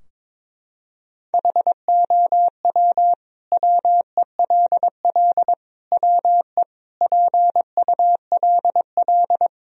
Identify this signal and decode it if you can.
Morse code, heard as a keyed tone, about 22 wpm, 700 Hz. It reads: HOW WELL WE PULL
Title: Sentences from Top 1000 Words - Review Part 2 - 22wpm